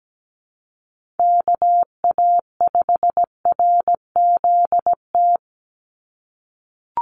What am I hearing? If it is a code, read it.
Morse code, heard as a keyed tone, about 17 wpm, 700 Hz. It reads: KA5RZT
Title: USA Call Signs 17wpm